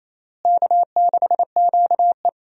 Morse code, heard as a keyed tone, about 28 wpm, 700 Hz. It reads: K6QE
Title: USA Call Signs - 4 Characters - Repeated 3x - 28wpm